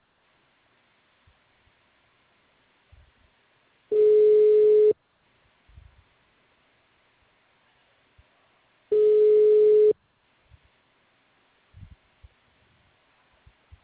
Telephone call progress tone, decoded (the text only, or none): ringback tone